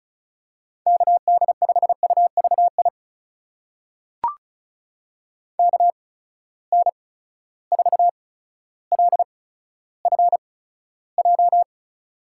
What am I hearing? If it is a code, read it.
Morse code, heard as a keyed tone, about 36 wpm, spaced as for 15 wpm, 700 Hz. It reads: KD5UVI KN4LFJ
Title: USA Call Signs - 6 Characters - 8x Character Spacing - 35wpm